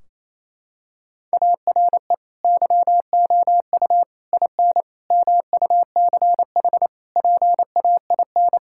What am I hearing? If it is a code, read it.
Morse code, heard as a keyed tone, about 28 wpm, 700 Hz. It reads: ARE YOU IN MUCH PAIN
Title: Sentences from Top 1600 Words 28wpm